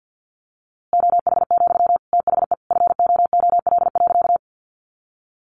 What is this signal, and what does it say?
Morse code, heard as a keyed tone, about 50 wpm, 700 Hz. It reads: OH, THE FOOL.
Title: A Sound Of Thunder 50wpm